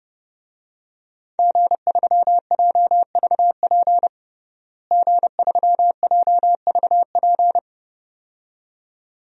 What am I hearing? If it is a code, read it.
Morse code, heard as a keyed tone, about 32 wpm, 700 Hz. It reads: G3JVP G3JVP
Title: DXCC-100 Call Signs - 5 Characters - Repeated 2x - 30wpm